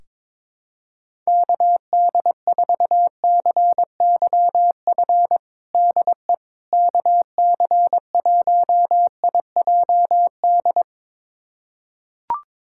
Morse code, heard as a keyed tone, about 22 wpm, 700 Hz. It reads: KD4CYF DE KC1IJD
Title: USA Callsign DE Callsign - 6 Characters - 22wpm